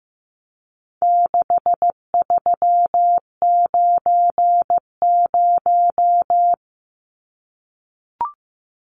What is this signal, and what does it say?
Morse code, heard as a keyed tone, about 15 wpm, 700 Hz. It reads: 6390
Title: Numbers - 4-Digit 15wpm